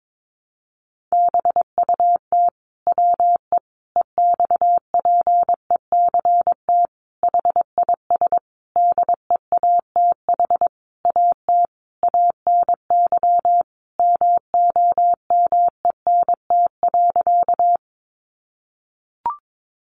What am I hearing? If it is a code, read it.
Morse code, heard as a keyed tone, about 22 wpm, 700 Hz. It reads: BUT WE EXPECT HIS DEATH AT ANY MOMENT.
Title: Sentences from Top 1100 Words 22wpm